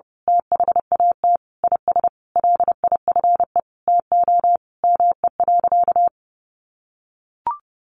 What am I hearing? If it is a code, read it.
Morse code, heard as a keyed tone, about 30 wpm, 700 Hz. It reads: THAT IS LIFE TO ME.
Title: Sentences from Top 500 Words 30wpm